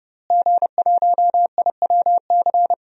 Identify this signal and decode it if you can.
Morse code, heard as a keyed tone, about 32 wpm, 700 Hz. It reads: G1IWC
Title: DXCC-100 Call Signs - 5 Characters - Repeated 3x - 30wpm